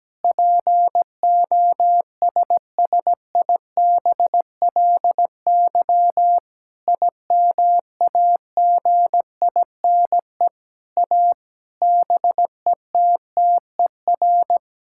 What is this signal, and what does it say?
Morse code, heard as a keyed tone, about 17 wpm, 700 Hz. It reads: POSSIBLY IMAGINE A BETTER
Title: Sentences from Top 1600 Words 17wpm